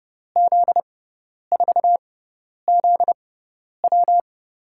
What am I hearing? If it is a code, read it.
Morse code, heard as a keyed tone, about 30 wpm, spaced as for 10 wpm, 700 Hz. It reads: Z4ZW
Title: USA Call Signs - 6x Character Spacing - 30wpm